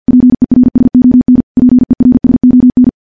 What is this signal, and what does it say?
Morse code, heard as a keyed tone, about 50 wpm, 250 Hz. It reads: PERSON PERSON
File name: morse / morse-training-50WPM